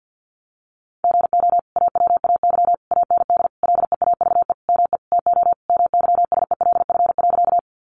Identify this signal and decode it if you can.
Morse code, heard as a keyed tone, about 50 wpm, 700 Hz. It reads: GO AWAY AND LEAVE ME TO MYSELF.
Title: Sentences from Top 1300 Words 50wpm